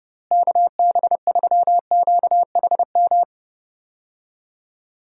Morse code, heard as a keyed tone, about 30 wpm, 700 Hz. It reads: KB3QHM
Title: USA Call Signs - 6 Characters - Repeated 3x - 30wpm